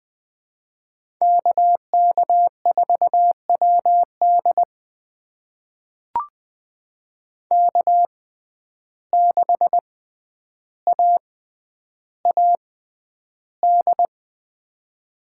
Morse code, heard as a keyed tone, about 20 wpm, spaced as for 10 wpm, 700 Hz. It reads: KK4WD K6AAD
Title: USA Call Signs - 5 Characters - 6x Character Spacing - 20wpm